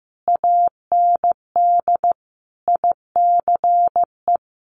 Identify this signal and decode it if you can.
Morse code, heard as a keyed tone, about 15 wpm, 700 Hz. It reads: AND ICE
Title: Sentences from Top 1000 Words 15wpm